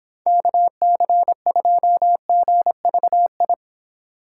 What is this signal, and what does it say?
Morse code, heard as a keyed tone, about 26 wpm, 700 Hz. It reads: KC2GVI